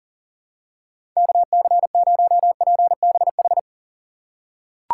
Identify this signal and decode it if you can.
Morse code, heard as a keyed tone, about 38 wpm, 700 Hz. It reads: KC0PBH
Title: USA Call Signs - 6 Characters - 6x Character Spacing - 40wpm